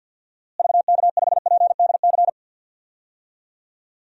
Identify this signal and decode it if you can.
Morse code, heard as a keyed tone, about 47 wpm, 700 Hz. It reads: VK5PDC
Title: DXCC-100 Call Signs - 6 Characters - Repeated 2x - 50wpm